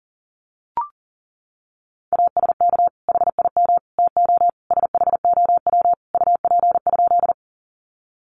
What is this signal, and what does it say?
Morse code, heard as a keyed tone, about 40 wpm, 700 Hz. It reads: ASK HIM TO SHOW UP?
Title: Sentences from Top 200 Words 40wpm